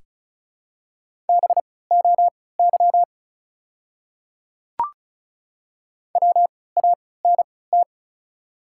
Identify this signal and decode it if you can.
Morse code, heard as a keyed tone, about 36 wpm, spaced as for 20 wpm, 700 Hz. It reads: BOY WANT
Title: 3x Character Spacing - Top 200 Words - 35wpm